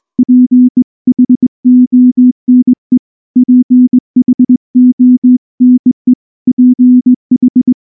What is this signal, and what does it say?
Morse code, heard as a keyed tone, about 20 wpm, 250 Hz. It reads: PHONE PHONE PH
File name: morse / morse-training-20WPM-3